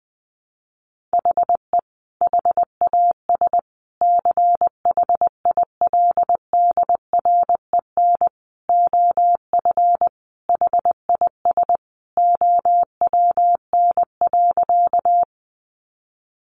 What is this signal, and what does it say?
Morse code, heard as a keyed tone, about 20 wpm, 700 Hz. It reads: HE HAS CHILDREN OF HIS OWN.
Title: Sentences from Top 300 Words 20wpm